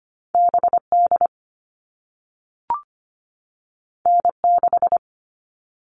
Morse code, heard as a keyed tone, about 25 wpm, 700 Hz. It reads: BD N6
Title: All Cut Numbers - 2-Digit 25wpm